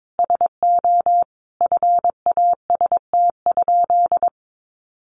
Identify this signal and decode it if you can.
Morse code, heard as a keyed tone, about 22 wpm, 700 Hz. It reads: SO FAST?
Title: Sentences from Top 1300 Words 22wpm